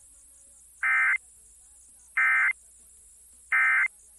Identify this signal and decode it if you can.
Emergency Alert System (SAME) tones, heard as harsh buzzing bursts. It is heard from 0.8 to 3.9 s.